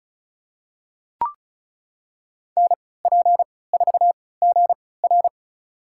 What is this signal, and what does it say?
Morse code, heard as a keyed tone, about 34 wpm, spaced as for 20 wpm, 700 Hz. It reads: NP4GR
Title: USA Call Signs - 5 Characters - 3x Character Spacing - 35wpm